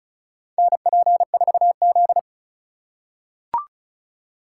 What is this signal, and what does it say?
Morse code, heard as a keyed tone, about 36 wpm, 700 Hz. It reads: NP4Z
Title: USA Call Signs - 4 Characters - 2x Character Spacing - 35wpm